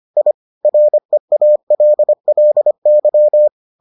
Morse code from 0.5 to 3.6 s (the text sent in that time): REALLY